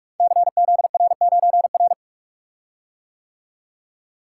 Morse code, heard as a keyed tone, about 44 wpm, 700 Hz. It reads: CZR9R
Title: Code Groups - 5 Letter:Numbers - 45wpm